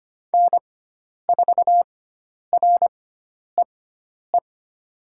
Morse code, heard as a keyed tone, about 25 wpm, spaced as for 10 wpm, 700 Hz. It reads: N4REE
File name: Lesson 16g - Letter c - Call Signs 25wpm